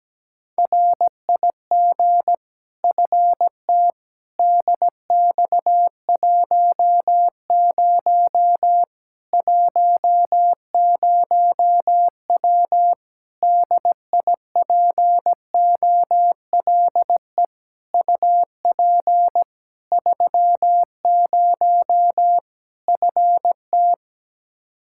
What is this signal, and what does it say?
Morse code, heard as a keyed tone, about 17 wpm, 700 Hz. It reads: RIG FT DX10 10W DIPOLE UP 30 FT
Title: Practice QSO - Exchange 2 - Info - 17wpm